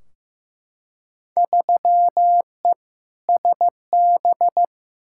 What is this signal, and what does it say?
Morse code, heard as a keyed tone, about 15 wpm, 700 Hz. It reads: 3E SB
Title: Field Day Exchanges Encore - 15wpm